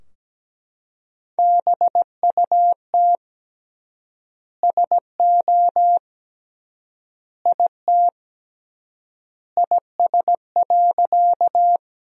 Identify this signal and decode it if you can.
Morse code, heard as a keyed tone, about 17 wpm, 700 Hz. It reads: BUT SO IT IS.
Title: 3x Word-Spacing - Sentences from Top 100 Words - 17wpm